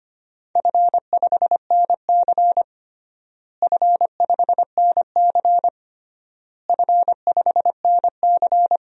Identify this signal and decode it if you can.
Morse code, heard as a keyed tone, about 25 wpm, 700 Hz. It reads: F5NC F5NC F5NC
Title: DXCC-100 Call Signs - 4 Characters - Repeated 3x - 25wpm